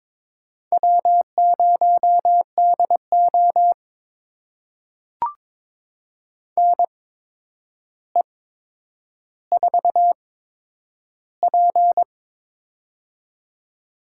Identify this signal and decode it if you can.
Morse code, heard as a keyed tone, about 22 wpm, spaced as for 9 wpm, 700 Hz. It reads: W0DO NE4P
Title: USA Call Signs - 8x Character Spacing - 22wpm